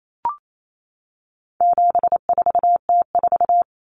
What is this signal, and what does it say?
Morse code, heard as a keyed tone, about 28 wpm, 700 Hz. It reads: 74T4
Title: Common Cut Numbers - 4-Digit 28wpm